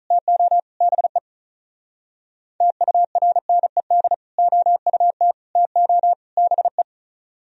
Morse code, heard as a keyed tone, about 44 wpm, 700 Hz. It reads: TO BE TURNED OUT TO BE